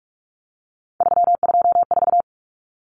Morse code, heard as a keyed tone, about 44 wpm, 700 Hz. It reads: 324
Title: Numbers - 3-Digit 45wpm